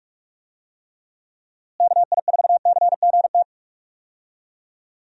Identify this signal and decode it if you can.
Morse code, heard as a keyed tone, about 46 wpm, 700 Hz. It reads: KI4CGT
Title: USA Call Signs - 6 Characters - Repeated 2x - 45wpm